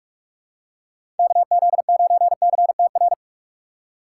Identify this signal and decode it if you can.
Morse code, heard as a keyed tone, about 47 wpm, 700 Hz. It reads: KZ9CTR
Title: USA Call Signs - 6 Characters - 3x Character Spacing - 45wpm